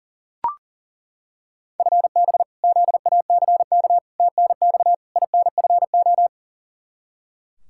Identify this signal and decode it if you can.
Morse code, heard as a keyed tone, about 40 wpm, 700 Hz. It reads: FB ZACK TNX INFO
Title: Practice QSO - Exchange 3 - More Info - 40wpm